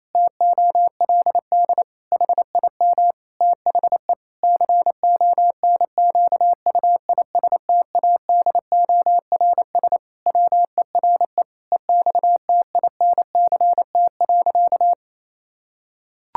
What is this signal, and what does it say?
Morse code, heard as a keyed tone, about 28 wpm, 700 Hz. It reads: TOLD HIM THE CONQUISTADORS WERE EXTINCT.
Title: Pangram Challenge v2 - 28wpm